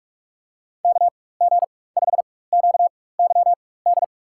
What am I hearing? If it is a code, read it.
Morse code, heard as a keyed tone, about 48 wpm, spaced as for 21 wpm, 700 Hz. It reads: KG5QYD
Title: USA Call Signs - 4x Character Spacing - 45wpm